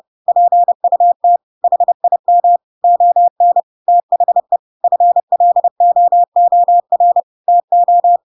Morse code, heard as a keyed tone, about 30 wpm, 700 Hz. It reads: PUT HIM ON THE FLOOR TO
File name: Pinocchio / Pinocchio ch3 30 30 ik1hge